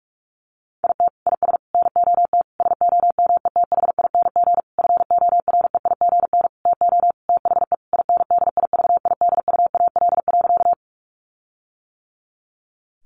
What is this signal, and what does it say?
Morse code, heard as a keyed tone, about 44 wpm, 700 Hz. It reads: IT IS NOT SOMETHING FOREIGN TO THE INDIVIDUAL.
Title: Sentences from Top 1400 Words 45wpm